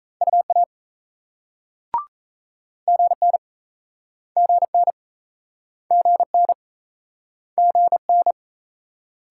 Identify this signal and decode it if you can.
Morse code, heard as a keyed tone, about 39 wpm, 700 Hz. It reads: UA GN GN GN GN